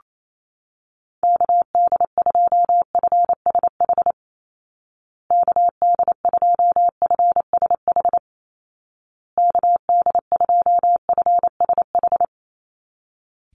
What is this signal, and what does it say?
Morse code, heard as a keyed tone, about 28 wpm, 700 Hz. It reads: KD2FSH KD2FSH KD2FSH
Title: Call Signs Repeated 2x - 28wpm